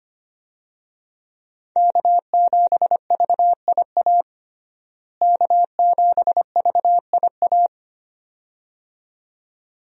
Morse code, heard as a keyed tone, about 25 wpm, 700 Hz. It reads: K7VIA K7VIA
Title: USA Call Signs - 5 Characters - Repeated 2x - 25wpm